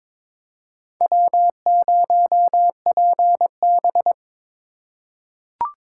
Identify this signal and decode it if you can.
Morse code, heard as a keyed tone, about 22 wpm, 700 Hz. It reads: W0PB
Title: USA Call Signs - 4 Characters - 3x Character Spacing - 22wpm